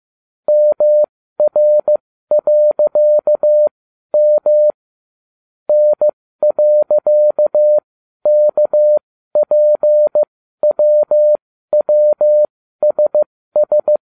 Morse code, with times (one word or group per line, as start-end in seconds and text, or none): 0.5-4.7 MR.M
5.7-14.0 N.KPWWSS